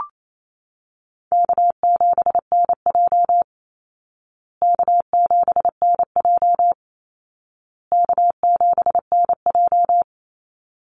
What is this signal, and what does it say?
Morse code, heard as a keyed tone, about 28 wpm, 700 Hz. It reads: K7NJ K7NJ K7NJ
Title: Easy Call Signs Repeated 3x - 28wpm